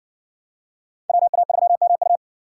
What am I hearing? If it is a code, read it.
Morse code, heard as a keyed tone, about 55 wpm, 700 Hz. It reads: WA3MU